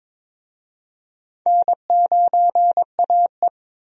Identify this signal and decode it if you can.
Morse code, heard as a keyed tone, about 22 wpm, 700 Hz. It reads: N9AE
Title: USA Call Signs - 4 Characters - 22wpm